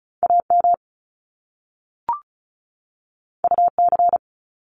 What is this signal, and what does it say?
Morse code, heard as a keyed tone, about 36 wpm, 700 Hz. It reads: AM UC
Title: Top Two Letter Combinations in Words 35wpm